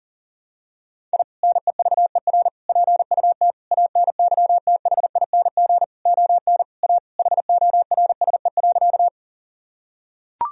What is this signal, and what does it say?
Morse code, heard as a keyed tone, about 40 wpm, 700 Hz. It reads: I NEVER PUT ANYTHING ON A HORSE.
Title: Sentences from Top 1000 Words - Review Part 1 - 40wpm